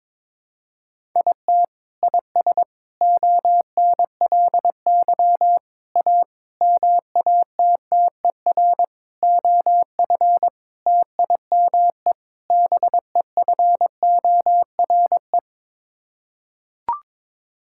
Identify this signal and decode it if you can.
Morse code, heard as a keyed tone, about 22 wpm, 700 Hz. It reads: IT IS ONLY A MATTER OF TIME BEFORE
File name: practice / 22wpm / Sets of 8 Words - 22wpm